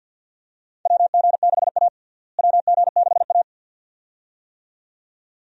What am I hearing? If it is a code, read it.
Morse code, heard as a keyed tone, about 54 wpm, 700 Hz. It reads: WG6A WG6A
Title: USA Call Signs - 4 Characters - Repeated 2x - 50wpm